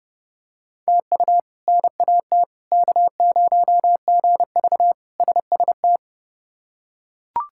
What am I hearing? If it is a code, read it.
Morse code, heard as a keyed tone, about 32 wpm, 700 Hz. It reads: TU NAT K0GV SST
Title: K1USN SST QSO Practice - 30wpm